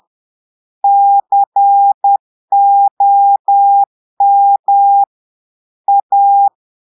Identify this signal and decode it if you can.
Morse code, heard as a keyed tone, about 10 wpm, 800 Hz. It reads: COM A
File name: Portugais - texte en clair 1 - vitesse - 10 mots minute